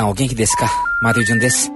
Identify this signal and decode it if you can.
Special information tone, with speech.